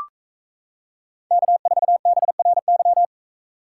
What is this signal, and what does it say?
Morse code, heard as a keyed tone, about 44 wpm, 700 Hz. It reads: K4BRY